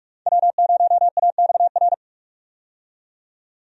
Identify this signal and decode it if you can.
Morse code, heard as a keyed tone, about 45 wpm, 700 Hz. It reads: W0AXR
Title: USA Call Signs - 5 Characters - 45wpm